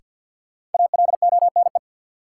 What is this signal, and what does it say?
Morse code, heard as a keyed tone, about 49 wpm, 700 Hz. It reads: ALONE
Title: Sentences from Top 1000 Words - New Word Drill - 50wpm